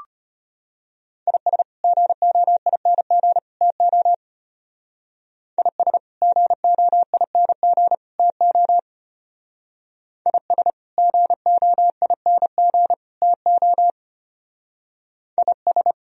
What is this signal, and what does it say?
Morse code, heard as a keyed tone, about 32 wpm, 700 Hz. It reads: IS GOING TO IS GOING TO IS GOING TO IS